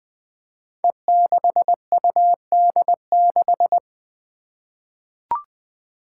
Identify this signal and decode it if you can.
Morse code, heard as a keyed tone, about 20 wpm, 700 Hz. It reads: E6UD6
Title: Five Challenging Letter-Number Combinations 20wpm